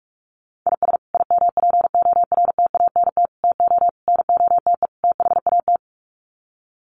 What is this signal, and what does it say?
Morse code, heard as a keyed tone, about 44 wpm, 700 Hz. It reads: IS IMPORTANT TO NOTE THAT
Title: Sets of 5 Words 45wpm